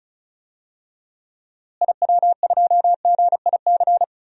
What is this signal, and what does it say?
Morse code, heard as a keyed tone, about 36 wpm, 700 Hz. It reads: IW2GIC
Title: DXCC-100 Call Signs - 6 Characters - Repeated 2x - 35wpm